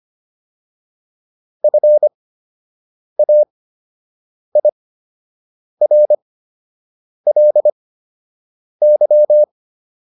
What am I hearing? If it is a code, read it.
Morse code, heard as a keyed tone, about 25 wpm, spaced as for 7 wpm, 600 Hz. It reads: FAIRLY
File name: LY6 07